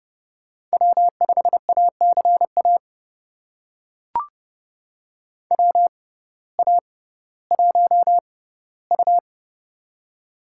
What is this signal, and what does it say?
Morse code, heard as a keyed tone, about 30 wpm, spaced as for 17 wpm, 700 Hz. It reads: W5ACA WA1U
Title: USA Call Signs - 6x Character Spacing - 30wpm